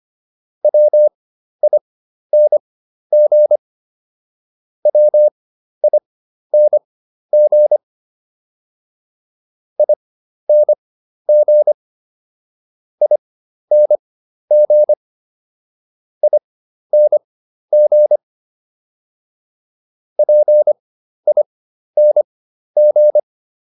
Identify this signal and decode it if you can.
Morse code, heard as a keyed tone, about 25 wpm, spaced as for 12 wpm, 600 Hz. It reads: WING WING ING ING ING PING